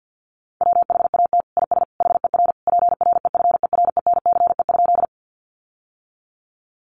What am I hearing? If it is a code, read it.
Morse code, heard as a keyed tone, about 48 wpm, 700 Hz. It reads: WHAT IS HER PREFERENCE?
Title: Sentences from Top 1500 Words 50wpm